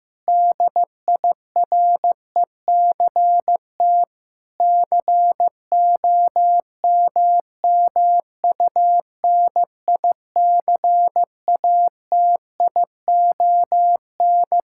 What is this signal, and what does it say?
Morse code, heard as a keyed tone, about 15 wpm, 700 Hz. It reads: DIRECT COMMUNICATION
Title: Sentences from Top 1000 Words - Review Part 6 - 15wpm